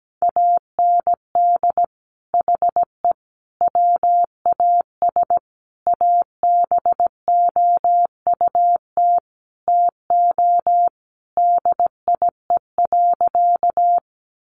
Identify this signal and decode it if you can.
Morse code, heard as a keyed tone, about 17 wpm, 700 Hz. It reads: AND HE WAS ABOUT TO DIE.
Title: Sentences from Top 1300 Words 17wpm